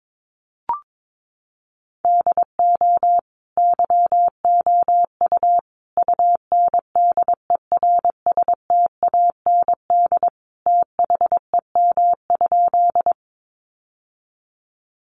Morse code, heard as a keyed tone, about 22 wpm, 700 Hz. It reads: DO YOU UNDERSTAND THEM?
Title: Sentences from Top 600 Words 22wpm